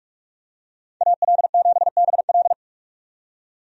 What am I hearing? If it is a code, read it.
Morse code, heard as a keyed tone, about 47 wpm, 700 Hz. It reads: AL7BL
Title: USA Call Signs - 5 Characters - 45wpm